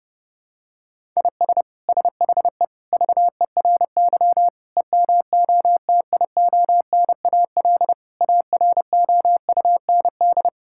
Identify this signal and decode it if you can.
Morse code, heard as a keyed tone, about 32 wpm, 700 Hz. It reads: IS SHE VERY EMOTIONAL AROUND
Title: Sentences from Top 1000 Words - Review Part 5 - 30wpm